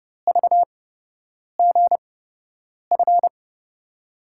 Morse code, heard as a keyed tone, about 30 wpm, spaced as for 8 wpm, 700 Hz. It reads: VGF
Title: USA Call Signs - 5 Characters - 8x Character Spacing - 30wpm